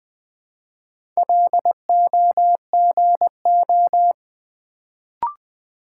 Morse code, heard as a keyed tone, about 20 wpm, 700 Hz. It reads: LOGO
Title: Build A Word - 1 Letter At A Time - 3 to 5-Letter Words - 20wpm